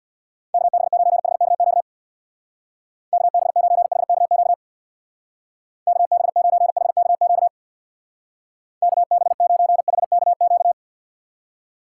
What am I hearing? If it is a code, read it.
Morse code, heard as a keyed tone, about 61 wpm, 700 Hz. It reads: KB9HKQ KB9HKQ KB9HKQ KB9HKQ
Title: Speed Racing - USA 2x3 Call Signs 50wpm